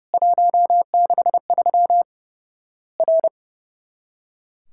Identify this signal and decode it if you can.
Morse code, heard as a keyed tone, about 30 wpm, 700 Hz. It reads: 163 R
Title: Worked All Europe DX Contest - QTC Exchange - 30wpm